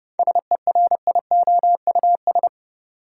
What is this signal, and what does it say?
Morse code, heard as a keyed tone, about 30 wpm, 700 Hz. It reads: SERIOUS
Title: Sentences from Top 1100 Words - New Word Drill Alt - 30wpm